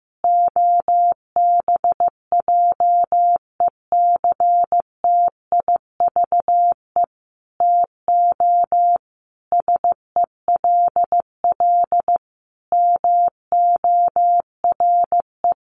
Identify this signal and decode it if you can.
Morse code, heard as a keyed tone, about 15 wpm, 700 Hz. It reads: OBJECTIVE TO SELL MORE
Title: Sentences from Top 1200 Words 15wpm